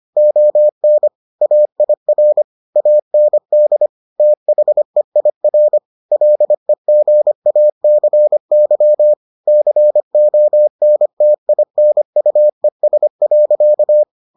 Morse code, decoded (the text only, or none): ON AIR AND THEIR LEGACY CONTINUES.